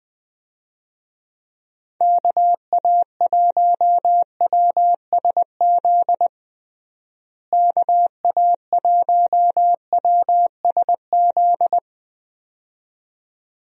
Morse code, heard as a keyed tone, about 21 wpm, 700 Hz. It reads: KA1WSZ KA1WSZ
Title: USA Call Signs - 6 Characters - Repeated 2x - 20wpm